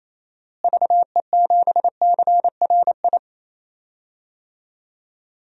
Morse code, heard as a keyed tone, about 28 wpm, 700 Hz. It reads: VE7CRI
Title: DXCC-100 Call Signs - 6 Characters - Repeated 3x - 28wpm